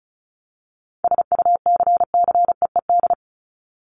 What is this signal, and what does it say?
Morse code, heard as a keyed tone, about 34 wpm, 700 Hz. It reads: SUCCEED
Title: Top 1501-1750 Words 35wpm